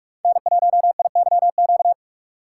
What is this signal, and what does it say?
Morse code, heard as a keyed tone, about 44 wpm, 700 Hz. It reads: N1IYQ
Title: USA Call Signs - 5 Characters - 6x Character Spacing - 45wpm